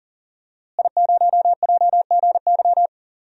Morse code, heard as a keyed tone, about 40 wpm, 700 Hz. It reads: I0JGY
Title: DXCC-100 Call Signs - 5 Characters - Repeated 2x - 40wpm